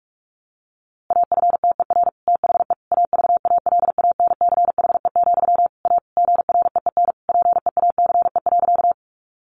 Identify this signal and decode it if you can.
Morse code, heard as a keyed tone, about 44 wpm, 700 Hz. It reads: AFTER THE AVALANCHE, A GREEN PEACE.
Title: A Sound Of Thunder 45wpm